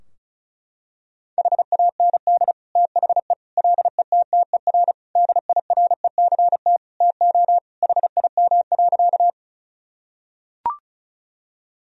Morse code, heard as a keyed tone, about 34 wpm, 700 Hz. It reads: HAND THE LETTER DIRECT TO HIM.
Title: Sentences from Top 1000 Words - Review Part 2 - 35wpm